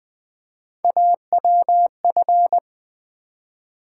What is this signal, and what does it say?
Morse code, heard as a keyed tone, about 20 wpm, 690 Hz. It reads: AWF
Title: Build A Word - 1 Letter At A Time - 5-Letter Words - 20wpm